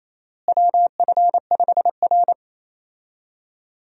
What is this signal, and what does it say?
Morse code, heard as a keyed tone, about 27 wpm, 700 Hz. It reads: WF5R
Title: USA Call Signs - 4 Characters - Repeated 3x - 28wpm